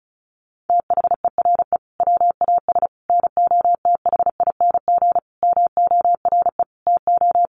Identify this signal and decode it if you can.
Morse code, heard as a keyed tone, about 34 wpm, 700 Hz. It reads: THERE WAS NOTHING MORE TO
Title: Sentences from Top 1100 Words 35wpm